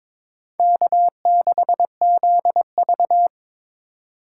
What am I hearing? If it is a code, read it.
Morse code, heard as a keyed tone, about 22 wpm, 700 Hz. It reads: K6ZV
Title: USA Call Signs - 4 Characters - 4x Character Spacing - 22wpm